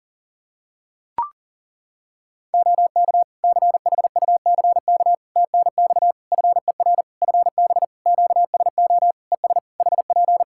Morse code, heard as a keyed tone, about 40 wpm, 700 Hz. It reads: OK CHUCK TNX FER FB QSO ES HP
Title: Practice QSO - Exchange 4 - Ending - 40wpm